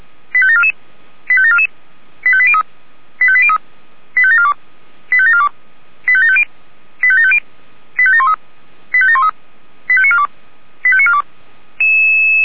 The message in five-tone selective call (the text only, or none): 85744 85744 85803 85803 85742 85742 85750 85750 85713 85713 85942 85942